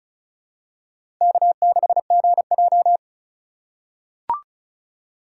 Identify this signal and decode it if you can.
Morse code, heard as a keyed tone, about 36 wpm, 700 Hz. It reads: K6GJ
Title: USA Call Signs - 4 Characters - 3x Character Spacing - 35wpm